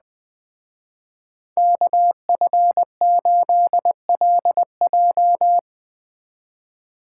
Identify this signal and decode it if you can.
Morse code, heard as a keyed tone, about 20 wpm, 700 Hz. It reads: KF8LJ
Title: Speed Racing - USA 2x2 Call Signs 17wpm